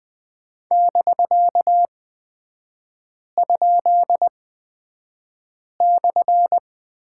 Morse code, heard as a keyed tone, about 20 wpm, spaced as for 5 wpm, 700 Hz. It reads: <BK>?/
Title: 255 Lesson 41e - Period - Three Characters 20wpm